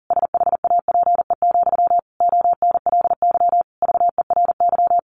Morse code, heard as a keyed tone, about 37 wpm, 700 Hz. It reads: SHAPE, ONLY VERY
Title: Sentences from Top 1400 Words 40wpm